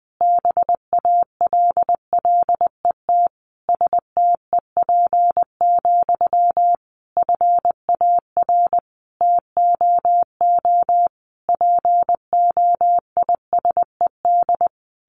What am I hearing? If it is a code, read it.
Morse code, heard as a keyed tone, about 19 wpm, 700 Hz. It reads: BALLET STEP, FAR TOO POISED
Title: A Sound Of Thunder 20wpm